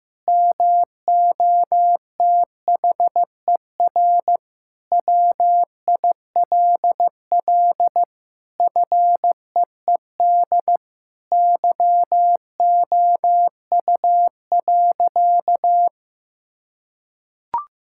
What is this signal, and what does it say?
Morse code, heard as a keyed tone, about 15 wpm, 700 Hz. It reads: MOTHER WILL FEED YOU.
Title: Sentences from Top 1600 Words 15wpm